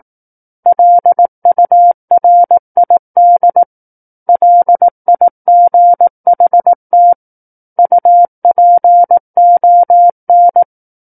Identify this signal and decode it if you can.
Morse code, heard as a keyed tone, about 18 wpm, 700 Hz. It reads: LURID LIGHT UPON
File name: 18 WPM